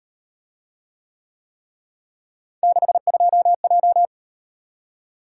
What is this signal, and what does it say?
Morse code, heard as a keyed tone, about 38 wpm, 700 Hz. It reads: 62J